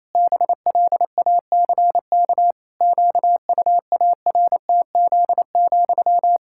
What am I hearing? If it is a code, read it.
Morse code, heard as a keyed tone, about 28 wpm, 700 Hz. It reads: BLACK QUARTZ,
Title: Pangram Challenge v2 - 28wpm